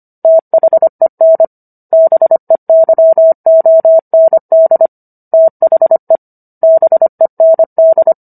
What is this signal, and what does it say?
Morse code, heard as a keyed tone, about 25 wpm, 650 Hz. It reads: THEN BEYOND THE BEND